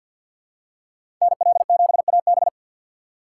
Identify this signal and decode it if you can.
Morse code, heard as a keyed tone, about 52 wpm, 700 Hz. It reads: NL7AB
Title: USA Call Signs - 5 Characters - 2x Character Spacing - 50wpm